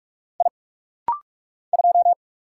Morse code, heard as a keyed tone, about 44 wpm, 700 Hz. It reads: I 2
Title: Single Letter-Number Alt - Rapid-Fire 45wpm